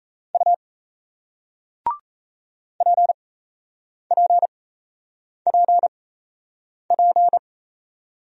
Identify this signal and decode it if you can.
Morse code, heard as a keyed tone, about 37 wpm, spaced as for 8 wpm, 700 Hz. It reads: U PPPP